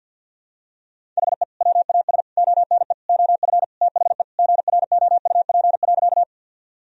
Morse code, heard as a keyed tone, about 51 wpm, 700 Hz. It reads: HE WAS ONE OF THE GROUP.
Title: Sentences from Top 1000 Words - Review Part 1 - 50wpm